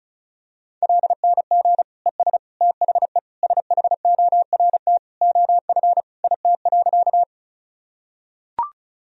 Morse code, heard as a keyed tone, about 36 wpm, 700 Hz. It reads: LNG ES THE SHORT OF IT.
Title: Sentences from Top 400 Words and Abbreviations 35wpm